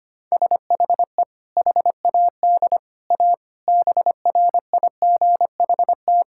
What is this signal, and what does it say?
Morse code, heard as a keyed tone, about 25 wpm, 700 Hz. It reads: SHE HAD A BRIGHT
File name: practice / 25wpm / Sentences with 2-Word Phrases - 25wpm